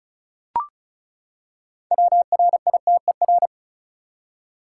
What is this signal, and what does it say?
Morse code, heard as a keyed tone, about 36 wpm, 700 Hz. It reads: WRITER
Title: Six Letter Words - 35wpm